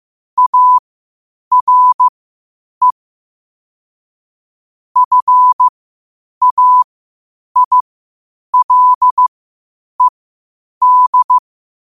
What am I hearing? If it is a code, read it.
Morse code, heard as a keyed tone, about 15 wpm, spaced as for 8 wpm, 1000 Hz. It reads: ARE FAILED
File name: qotdfiles / 8wpm